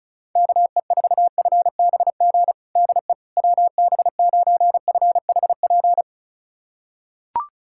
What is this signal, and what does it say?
Morse code, heard as a keyed tone, about 34 wpm, 700 Hz. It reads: KE4FBG DE WB9FHP
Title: USA Callsign DE Callsign - 6 Characters - 35wpm